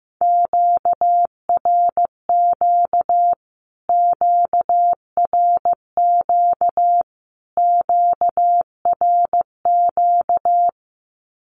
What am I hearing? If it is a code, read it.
Morse code, heard as a keyed tone, about 15 wpm, 700 Hz. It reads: QRQ QRQ QRQ
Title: Q-Codes Repeated 3x 15wpm